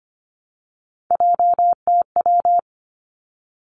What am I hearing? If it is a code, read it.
Morse code, heard as a keyed tone, about 25 wpm, 700 Hz. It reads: JTW
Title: Three Challenging Letter-Number Combinations 25wpm